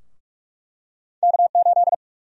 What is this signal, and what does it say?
Morse code, heard as a keyed tone, about 46 wpm, 700 Hz. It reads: K8
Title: USA Call Sign Prefixes - 45wpm